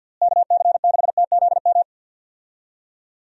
Morse code, heard as a keyed tone, about 52 wpm, 700 Hz. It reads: KC6TZM
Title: USA Call Signs - 6 Characters - Repeated 2x - 50wpm